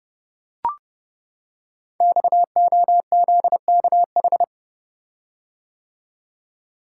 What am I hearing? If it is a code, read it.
Morse code, heard as a keyed tone, about 28 wpm, 700 Hz. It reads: XOZKH
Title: Code Groups - 5 Letter:Numbers - 30wpm